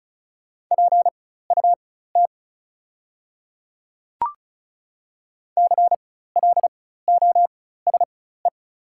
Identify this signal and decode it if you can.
Morse code, heard as a keyed tone, about 36 wpm, spaced as for 16 wpm, 690 Hz. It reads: PUT CLOSE
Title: 4x Character Spacing - Top 300 Words - 35wpm